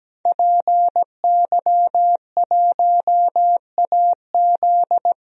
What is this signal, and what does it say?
Morse code, heard as a keyed tone, about 17 wpm, 700 Hz. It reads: PY1AZ
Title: DXCC-100 Call Signs - 5 Characters - 17wpm